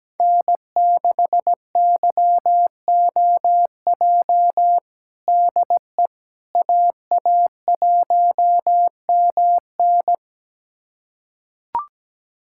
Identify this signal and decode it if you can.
Morse code, heard as a keyed tone, about 17 wpm, 700 Hz. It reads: N6YOJ DE AA1MN
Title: USA Callsign DE Callsign - 5 Characters - 17wpm